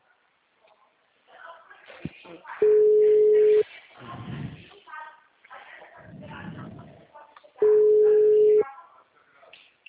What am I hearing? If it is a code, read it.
Ringback tone (Europe, 425 Hz), with speech.